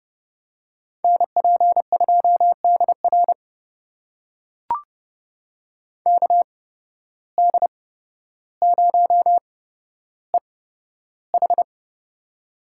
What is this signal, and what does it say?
Morse code, heard as a keyed tone, about 32 wpm, spaced as for 13 wpm, 700 Hz. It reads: NP2DR KD0EH
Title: USA Call Signs - 5 Characters - 8x Character Spacing - 30wpm